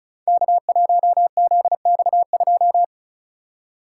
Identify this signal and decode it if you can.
Morse code, heard as a keyed tone, about 36 wpm, 700 Hz. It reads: K1ZX2
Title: Code Groups - 5 Letter:Numbers - 35wpm